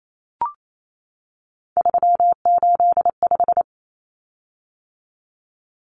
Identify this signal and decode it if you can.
Morse code, heard as a keyed tone, about 28 wpm, 700 Hz. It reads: 385
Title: Common Cut Numbers - 3-Digit 28wpm